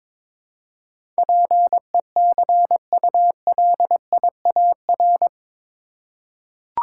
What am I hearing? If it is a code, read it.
Morse code, heard as a keyed tone, about 22 wpm, 700 Hz. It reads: PECULIAR
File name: practice / 22wpm / Eight Letter Words - 22wpm